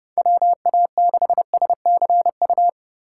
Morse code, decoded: WA6SCU

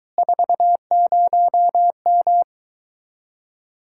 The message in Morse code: 40M

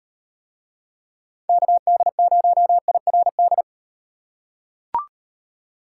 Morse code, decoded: KD0IRD

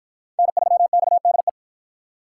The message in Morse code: N3XDE

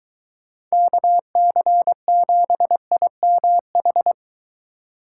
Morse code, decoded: KC7IMH

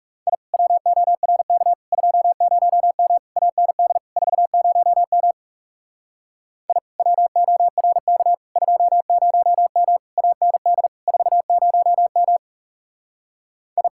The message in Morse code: I WORK 20M AND 40M I WORK 20M AND 40M I

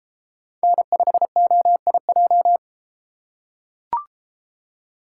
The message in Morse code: N5OIJ

